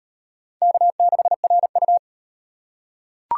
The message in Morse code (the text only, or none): K6RU